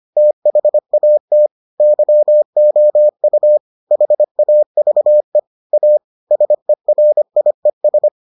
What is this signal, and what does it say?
Morse code, heard as a keyed tone, about 25 wpm, 600 Hz. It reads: THAT YOU HAVE A SERIES